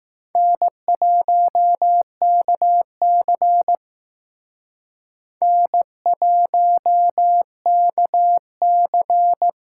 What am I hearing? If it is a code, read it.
Morse code, heard as a keyed tone, about 20 wpm, 700 Hz. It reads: N1KC N1KC